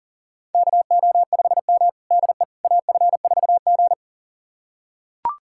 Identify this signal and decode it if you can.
Morse code, heard as a keyed tone, about 40 wpm, 700 Hz. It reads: KO5M DE AF4G
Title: USA Callsign DE Callsign - 4 Characters - 40wpm